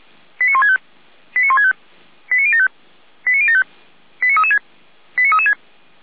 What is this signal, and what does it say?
Five-tone selective call (ZVEI), heard as quick runs of five tones, tones of about 70 ms. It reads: 89165 89165 89075 89075 89336 89336